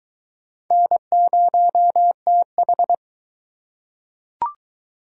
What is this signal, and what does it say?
Morse code, heard as a keyed tone, about 20 wpm, 700 Hz. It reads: N0TH